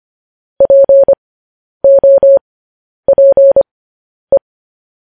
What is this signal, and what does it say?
Morse code, heard as a keyed tone, about 25 wpm, spaced as for 10 wpm, 550 Hz. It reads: POPE